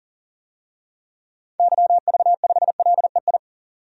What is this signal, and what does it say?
Morse code, heard as a keyed tone, about 37 wpm, 700 Hz. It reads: YV5LEI
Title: DXCC-100 Call Signs - 6 Characters - 40wpm